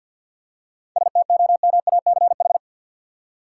Morse code, heard as a keyed tone, about 52 wpm, 700 Hz. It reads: STOMACH
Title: Seven Letter Words - 50wpm